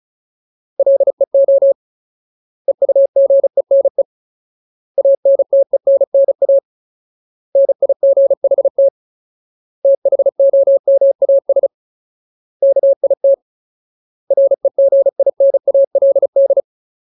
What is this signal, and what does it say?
Morse code, heard as a keyed tone, about 34 wpm, 550 Hz. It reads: LEO EUGENE ANTENNA NIGHT THOMAS KIT REGINALD
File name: Ultimate Word List / 35WPM Ultimate